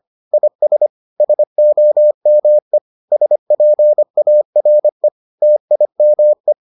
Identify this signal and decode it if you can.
Morse code, heard as a keyed tone, about 25 wpm, 600 Hz. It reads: IS SOME SPARE TIME